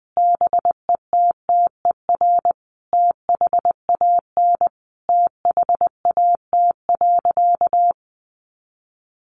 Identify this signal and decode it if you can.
Morse code, heard as a keyed tone, about 20 wpm, 700 Hz. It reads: BETTER THAN THAT.
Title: Sentences from Top 400 Words 20wpm